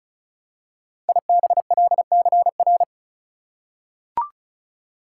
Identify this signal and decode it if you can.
Morse code, heard as a keyed tone, about 36 wpm, 700 Hz. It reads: IBLCR
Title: Code Groups - 5 Letter:Numbers - 35wpm